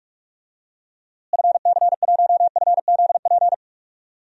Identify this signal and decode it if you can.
Morse code, heard as a keyed tone, about 46 wpm, 700 Hz. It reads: FC1FZP
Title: DXCC-100 Call Signs - 6 Characters - Repeated 3x - 45wpm